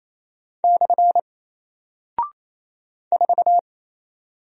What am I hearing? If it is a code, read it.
Morse code, heard as a keyed tone, about 27 wpm, 700 Hz. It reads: / 4
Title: Single Character (Letter:Number:Punctuation:Prosign) 28wpm